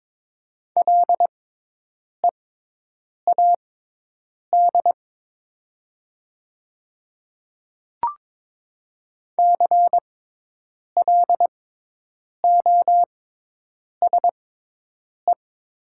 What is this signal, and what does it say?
Morse code, heard as a keyed tone, about 22 wpm, spaced as for 8 wpm, 700 Hz. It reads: LEAD CLOSE
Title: 6x Character Spacing - Top 500 Words - 22wpm